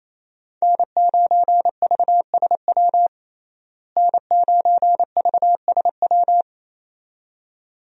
Morse code, heard as a keyed tone, about 27 wpm, 700 Hz. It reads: N9VSW N9VSW
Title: USA Call Signs - 5 Characters - Repeated 2x - 28wpm